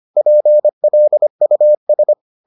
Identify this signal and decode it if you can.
Morse code, heard as a keyed tone, about 25 wpm, 600 Hz. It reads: PLUS